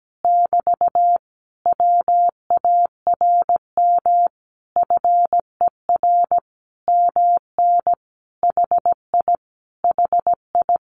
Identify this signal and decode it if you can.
Morse code, heard as a keyed tone, about 17 wpm, 700 Hz. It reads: <BT> WARM FER MN HI HI
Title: Common QSO Phrases Encore 17wpm